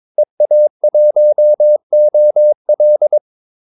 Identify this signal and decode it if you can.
Morse code, heard as a keyed tone, about 22 wpm, 600 Hz. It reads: EA1OL
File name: Callsigns-22wpm